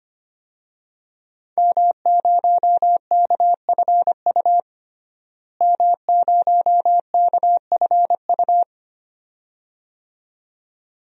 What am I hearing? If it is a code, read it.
Morse code, heard as a keyed tone, about 25 wpm, 700 Hz. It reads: M0KFU M0KFU
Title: DXCC-100 Call Signs - 5 Characters - Repeated 2x - 25wpm